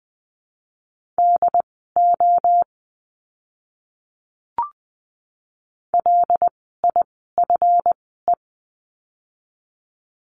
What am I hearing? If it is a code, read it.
Morse code, heard as a keyed tone, about 20 wpm, spaced as for 14 wpm, 700 Hz. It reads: DO LIFE
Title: 192 Lesson 31f - Letter v - Words 20wpm